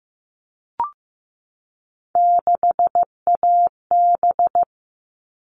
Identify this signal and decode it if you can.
Morse code, heard as a keyed tone, about 15 wpm, 700 Hz. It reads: 6AB
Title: All Cut Numbers - 3-Digit 15wpm